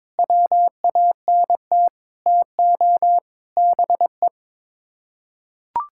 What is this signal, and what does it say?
Morse code, heard as a keyed tone, about 21 wpm, 700 Hz. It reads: WANT TO BE
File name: practice / 22wpm / Sets of 6 Words - 22wpm